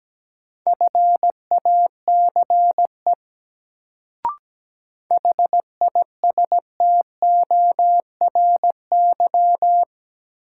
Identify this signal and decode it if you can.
Morse code, heard as a keyed tone, about 17 wpm, 700 Hz. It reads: FACE HISTORY
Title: Sets of 3 Words - New Word Drill - 17wpm